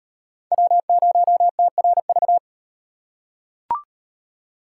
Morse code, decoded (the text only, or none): W0TRV